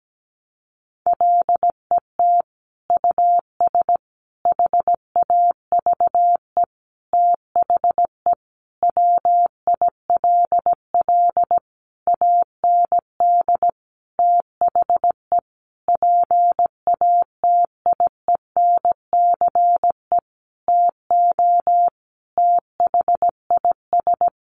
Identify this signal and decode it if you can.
Morse code, heard as a keyed tone, about 17 wpm, 700 Hz. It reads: LET US HAVE THE WILL AND THE PATIENCE TO THIS
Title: Sentences from Top 1300 Words 17wpm